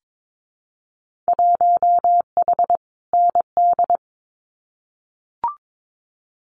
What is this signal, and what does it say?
Morse code, heard as a keyed tone, about 22 wpm, 700 Hz. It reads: 1H ND
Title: Winter Field Day Exchanges 22wpm